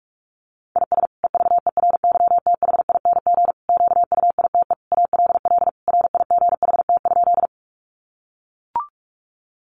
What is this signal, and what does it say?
Morse code, heard as a keyed tone, about 44 wpm, 700 Hz. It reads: IS EVERYTHING QUITE ALL RIGHT?
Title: Sentences from Top 700 Words 45wpm